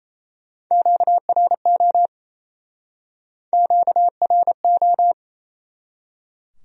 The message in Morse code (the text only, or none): QRO QRO